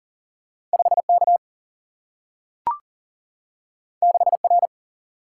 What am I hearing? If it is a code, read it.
Morse code, heard as a keyed tone, about 40 wpm, 700 Hz. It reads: 5K 6R
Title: Single-Letter Single-Number 40wpm